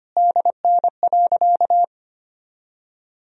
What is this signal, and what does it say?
Morse code, heard as a keyed tone, about 25 wpm, 700 Hz. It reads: DN.
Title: Sentences from Top 500 Words and Abbreviations 25wpm